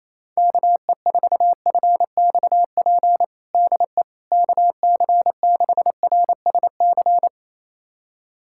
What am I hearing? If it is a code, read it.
Morse code, heard as a keyed tone, about 28 wpm, 700 Hz. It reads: KE4FXP DE KC6RSC
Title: USA Callsign DE Callsign - 6 Characters - 28wpm